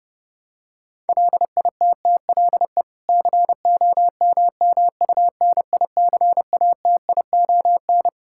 Morse code, heard as a keyed tone, about 32 wpm, 700 Hz. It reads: LITTLE COMMUNICATION
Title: Sentences from Top 1000 Words - Review Part 6 - 30wpm